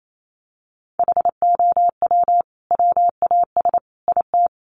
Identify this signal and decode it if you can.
Morse code, heard as a keyed tone, about 28 wpm, 700 Hz. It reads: HOW WAS IT
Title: Sentences from Top 500 Words 28wpm